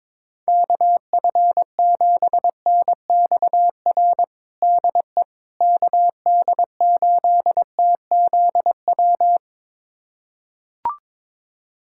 Morse code, heard as a keyed tone, about 23 wpm, 700 Hz. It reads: KF7NXR DE KD8TZW
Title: USA Callsign DE Callsign - 6 Characters - 22wpm